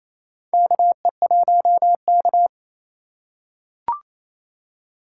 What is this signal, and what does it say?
Morse code, heard as a keyed tone, about 28 wpm, 700 Hz. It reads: KE1K
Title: USA Call Signs - 4 Characters - 2x Character Spacing - 28wpm